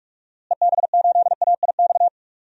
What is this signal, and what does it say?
Morse code, heard as a keyed tone, about 46 wpm, 700 Hz. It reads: EB8AIX
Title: DXCC-100 Call Signs - 6 Characters - Repeated 3x - 45wpm